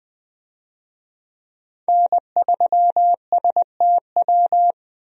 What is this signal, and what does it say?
Morse code, heard as a keyed tone, about 20 wpm, 700 Hz. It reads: N3STW
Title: USA Call Signs - 5 Characters - 20wpm